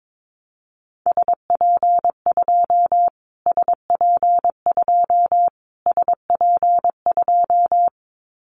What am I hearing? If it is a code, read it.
Morse code, heard as a keyed tone, about 22 wpm, 700 Hz. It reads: SP2 SP2 SP2
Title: Callsign Prefixes for Top 25 DXCC - Repeated 3x - 22wpm